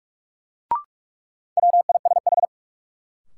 Morse code, heard as a keyed tone, about 45 wpm, 700 Hz. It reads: WISH
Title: Sentences from Top 1600 Words - New Word Drill - 45wpm